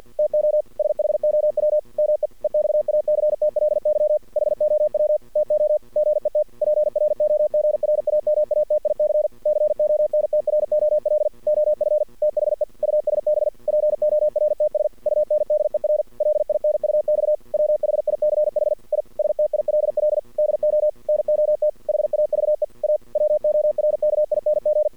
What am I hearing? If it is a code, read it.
Morse code, heard as a keyed tone, about 52 wpm, 600 Hz. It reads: TO SHOW ME EXACTLY HOW TO GET PROGRAMATIC CONTROL OF THE USB PORTS UNDER LINUX WHICH I STILL DO NOT HAVE A WORKING
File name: archived / CRSnet